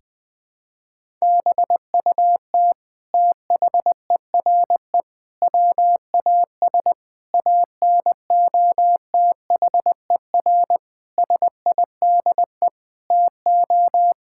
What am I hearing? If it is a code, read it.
Morse code, heard as a keyed tone, about 21 wpm, 700 Hz. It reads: BUT THERE WAS ANOTHER SIDE TO
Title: Sentences from Top 1600 Words 20wpm